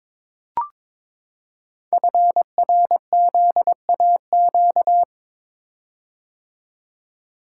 Morse code, heard as a keyed tone, about 22 wpm, 700 Hz. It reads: FRZAQ
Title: Code Groups - 5 Letters - 22wpm